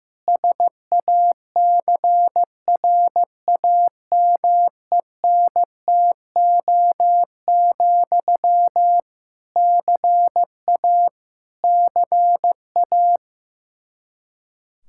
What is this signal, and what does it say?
Morse code, heard as a keyed tone, about 16 wpm, 700 Hz. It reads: SACRAMENTO, CA CA
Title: Practice QSO - Exchange 1 - Beginning - 15wpm